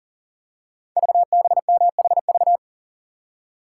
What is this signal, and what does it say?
Morse code, heard as a keyed tone, about 42 wpm, 690 Hz. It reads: VBMHV
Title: Code Groups - 5 Letters - 40wpm